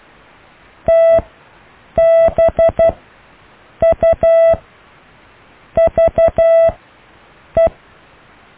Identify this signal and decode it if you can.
Morse code, heard as a keyed tone, about 12 wpm, spaced as for 6 wpm, 660 Hz. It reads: TBUVE
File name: cw / IJLUM+CDRO+EKTPV+BFHNW